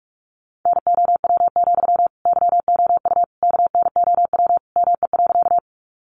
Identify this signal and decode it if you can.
Morse code, heard as a keyed tone, about 44 wpm, 700 Hz. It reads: NOW, YOU KNOW ME.
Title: Sentences from Top 200 Words 45wpm